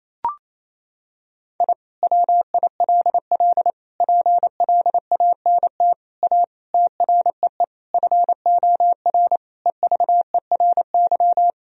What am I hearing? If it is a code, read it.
Morse code, heard as a keyed tone, about 28 wpm, 700 Hz. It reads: I WILL PLANT A TREE FOR EVERY
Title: Sentences from Top 1000 Words - Review Part 1 - 28wpm